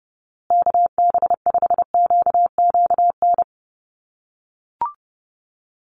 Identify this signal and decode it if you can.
Morse code, heard as a keyed tone, about 30 wpm, 700 Hz. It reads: KB5QQN
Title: USA Call Signs Repeated 3x 30wpm